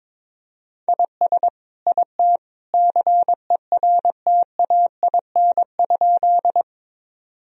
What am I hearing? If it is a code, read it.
Morse code, heard as a keyed tone, about 21 wpm, 700 Hz. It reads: IS IT CERTAIN?
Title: Sentences from Top 1000 Words - Review Part 2 - 22wpm